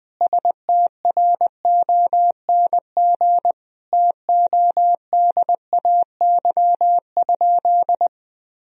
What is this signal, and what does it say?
Morse code, heard as a keyed tone, about 20 wpm, 700 Hz. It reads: STRONG TODAY?
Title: Sentences from Top 1000 Words - Review Part 2 - 20wpm